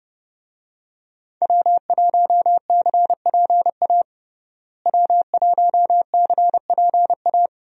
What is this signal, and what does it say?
Morse code, heard as a keyed tone, about 31 wpm, 700 Hz. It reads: W1CPA W1CPA
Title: USA Call Signs - 5 Characters - Repeated 2x - 30wpm